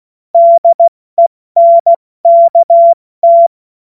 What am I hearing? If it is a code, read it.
Morse code, heard as a keyed tone, about 16 wpm, 680 Hz. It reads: DENKT
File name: Witz-16-September-2025-70-F80